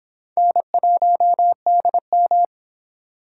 Morse code, heard as a keyed tone, about 30 wpm, 700 Hz. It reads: N1DM